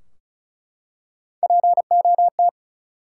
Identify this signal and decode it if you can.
Morse code, heard as a keyed tone, about 36 wpm, 700 Hz. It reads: POT
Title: Three Letter Words - 35wpm